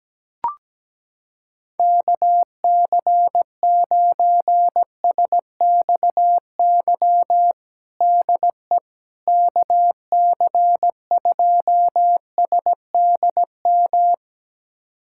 Morse code, heard as a keyed tone, about 17 wpm, 700 Hz. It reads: KC9SXY DE KC2SDM
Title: USA Callsign DE Callsign - 6 Characters - 17wpm